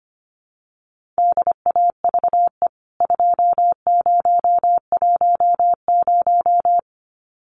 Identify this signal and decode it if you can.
Morse code, heard as a keyed tone, about 25 wpm, 700 Hz. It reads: DAVE 2010
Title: CWT Exchange 25wpm